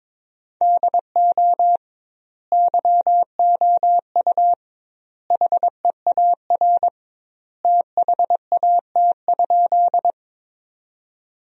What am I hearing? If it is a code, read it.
Morse code, heard as a keyed tone, about 22 wpm, 700 Hz. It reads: DO YOU HEAR THAT?
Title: 2x Word-Spacing - Sentences from Top 300 Words - 22wpm